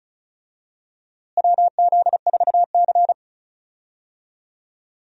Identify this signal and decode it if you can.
Morse code, heard as a keyed tone, about 35 wpm, 700 Hz. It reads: WZ4C
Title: USA Call Signs - 4 Characters - Repeated 2x - 35wpm